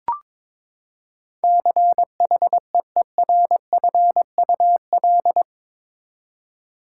Morse code, heard as a keyed tone, about 22 wpm, 700 Hz. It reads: CHEERFUL
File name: practice / 22wpm / Eight Letter Words - 22wpm